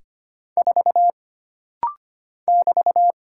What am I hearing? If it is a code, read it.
Morse code, heard as a keyed tone, about 25 wpm, 700 Hz. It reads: 4 <BT>
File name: Single Character (Letter:Number:Punctuation:Prosign) - Rapid-Fire - 25wpm